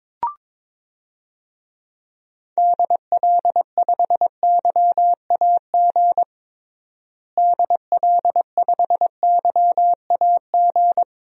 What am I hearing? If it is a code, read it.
Morse code, heard as a keyed tone, about 22 wpm, 700 Hz. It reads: DL5YAG DL5YAG
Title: DXCC-100 Call Signs - 6 Characters - Repeated 3x - 22wpm